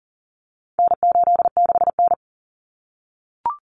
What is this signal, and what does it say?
Morse code, heard as a keyed tone, about 40 wpm, 700 Hz. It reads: N86N
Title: Common Cut Numbers - 4-Digit 40wpm